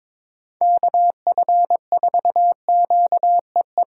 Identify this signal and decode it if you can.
Morse code, heard as a keyed tone, about 22 wpm, 700 Hz. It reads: KF4QEE
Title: USA Call Signs - 6 Characters - 3x Character Spacing - 22wpm